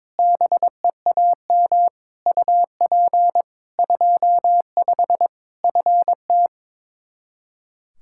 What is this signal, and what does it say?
Morse code, heard as a keyed tone, about 22 wpm, 700 Hz. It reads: BEAM UP 25 FT
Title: Practice QSO - Exchange 2 - Info - 22wpm